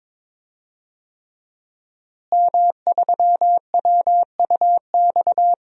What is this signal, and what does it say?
Morse code, heard as a keyed tone, about 22 wpm, 700 Hz. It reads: M3WUX
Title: DXCC-100 Call Signs - 5 Characters - Repeated 3x - 22wpm